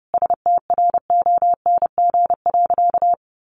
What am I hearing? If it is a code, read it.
Morse code, heard as a keyed tone, about 30 wpm, 700 Hz. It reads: STRONG.
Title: Sentences from Top 1400 Words 30wpm